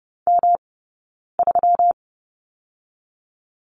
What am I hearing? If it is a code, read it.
Morse code, heard as a keyed tone, about 30 wpm, spaced as for 9 wpm, 700 Hz. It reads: M3
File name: Lesson 20d - Number 3 - Two Characters 30wpm